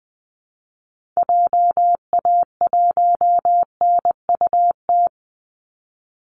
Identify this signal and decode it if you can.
Morse code, heard as a keyed tone, about 20 wpm, 700 Hz. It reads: JA1NUT
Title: Call Signs Repeated 2x - 20wpm